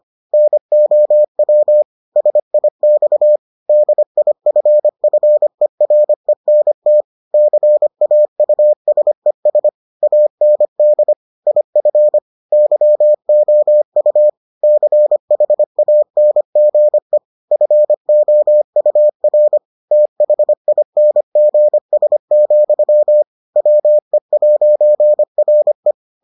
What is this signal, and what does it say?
Morse code, heard as a keyed tone, about 25 wpm, 600 Hz. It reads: NOW SIX DIFFERENT CAUSES AND IF YOU CHANGE FOUR THINGS, WE'RE